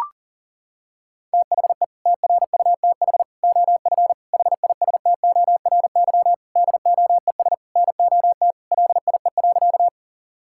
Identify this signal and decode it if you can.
Morse code, heard as a keyed tone, about 40 wpm, 700 Hz. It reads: THE TRUTH OF HISTORY DOES NOT LIE.
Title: Sentences from Top 1600 Words 40wpm